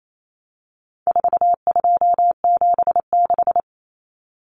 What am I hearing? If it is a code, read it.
Morse code, heard as a keyed tone, about 28 wpm, 700 Hz. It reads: 4276
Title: Common Cut Numbers - 4-Digit 28wpm